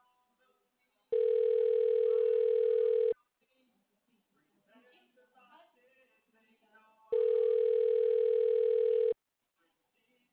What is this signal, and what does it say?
Ringback tone (North America), 440 + 480 Hz.